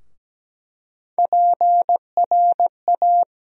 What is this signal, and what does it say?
Morse code, heard as a keyed tone, about 17 wpm, 700 Hz. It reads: PRA
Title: USA Call Sign Postfixes - 3 Letters 17wpm